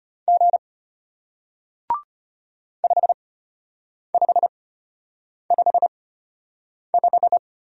Morse code, heard as a keyed tone, about 34 wpm, spaced as for 8 wpm, 700 Hz. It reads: G 5555